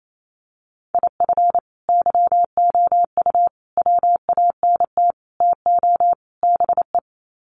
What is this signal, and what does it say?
Morse code, heard as a keyed tone, about 28 wpm, 700 Hz. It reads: IF YOU WANT TO BE
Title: Sets of 5 Words Encore 28wpm